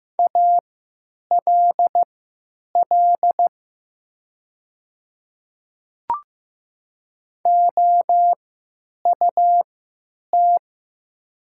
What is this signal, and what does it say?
Morse code, heard as a keyed tone, about 15 wpm, spaced as for 9 wpm, 700 Hz. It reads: ALL OUT
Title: 3x Character Spacing - Top 100 Words - 15wpm